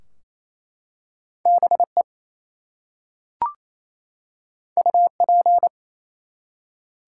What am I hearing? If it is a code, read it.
Morse code, heard as a keyed tone, about 27 wpm, 700 Hz. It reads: BE UP
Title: Two Letter Words - 28wpm